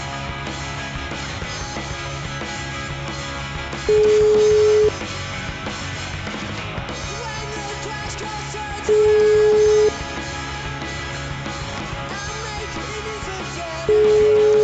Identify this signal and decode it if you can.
Ringback tone (Europe, 425 Hz), with noise.